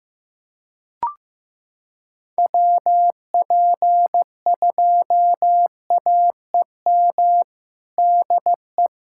Morse code, heard as a keyed tone, about 16 wpm, 700 Hz. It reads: WP2AEM DE
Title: USA Callsign DE Callsign - 6 Characters - 15wpm